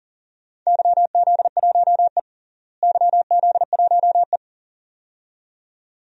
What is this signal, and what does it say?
Morse code, heard as a keyed tone, about 40 wpm, 700 Hz. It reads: YZ1E YZ1E
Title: DXCC-100 Call Signs - 4 Characters - Repeated 2x - 40wpm